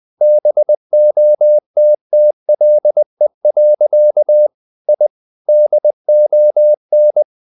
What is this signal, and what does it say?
Morse code, heard as a keyed tone, about 20 wpm, 600 Hz. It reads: BOTTLE. I DON
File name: the old man and the sea / the old man and the sea part 1 20wpm